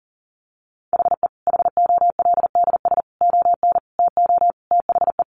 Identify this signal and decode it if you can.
Morse code, heard as a keyed tone, about 40 wpm, 700 Hz. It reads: HE HOLDS ON TO THE
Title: Additional Easy Phrases 40wpm